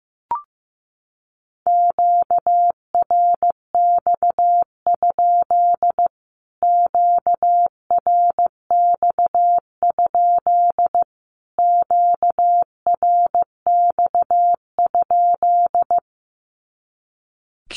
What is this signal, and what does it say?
Morse code, heard as a keyed tone, about 15 wpm, 700 Hz. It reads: QRX? QRX? QRX?
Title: Q-Codes Repeated 3x 15wpm